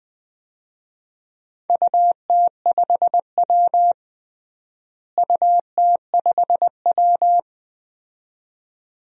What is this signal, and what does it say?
Morse code, heard as a keyed tone, about 21 wpm, 700 Hz. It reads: UT5W UT5W
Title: DXCC-100 Call Signs - 4 Characters - Repeated 2x - 20wpm